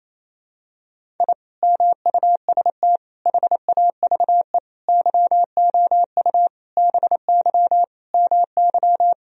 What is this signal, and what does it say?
Morse code, heard as a keyed tone, about 28 wpm, 700 Hz. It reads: I MUST HAVE YOU BY MY
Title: Sentences from Top 1000 Words - Review Part 1 - 28wpm